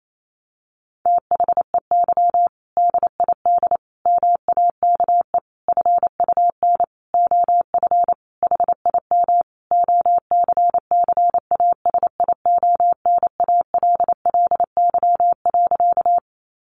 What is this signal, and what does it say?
Morse code, heard as a keyed tone, about 28 wpm, 700 Hz. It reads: THEY DID MAKE FUN OF HIM OCCASIONALLY.
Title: Sentences from Top 1300 Words 28wpm